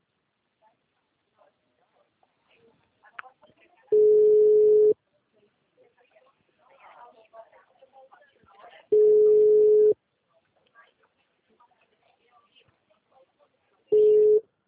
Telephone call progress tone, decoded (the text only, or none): ringback tone